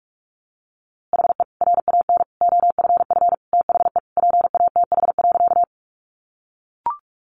Morse code, heard as a keyed tone, about 44 wpm, 700 Hz. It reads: HE RAN OFF THE PATH.
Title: A Sound Of Thunder 45wpm